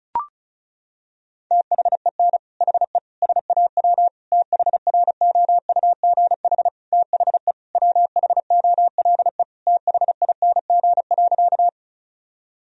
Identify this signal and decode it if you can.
Morse code, heard as a keyed tone, about 36 wpm, 690 Hz. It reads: THEN HE SAW THROUGH THE WHOLE THING.
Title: Sentences from Top 1000 Words - Review Part 1 - 35wpm